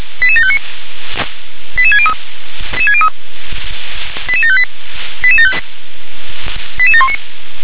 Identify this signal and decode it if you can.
Five-tone selective call (ZVEI), heard as quick runs of five tones, tones of about 70 ms. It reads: 88759 88603 88758 88719